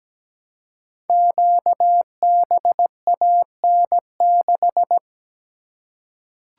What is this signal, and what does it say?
Morse code, heard as a keyed tone, about 17 wpm, 700 Hz. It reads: QBAN6
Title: Code Groups - 5 Letter:Numbers - 17wpm